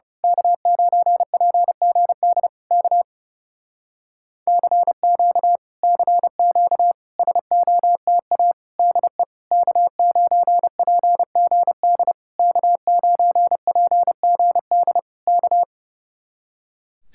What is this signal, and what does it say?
Morse code, heard as a keyed tone, about 28 wpm, 700 Hz. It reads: K9PGD K CQ CQ SOTA DE K9PGD K9PGD K
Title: Speed-Racing Common QSO Phrases - 30wpm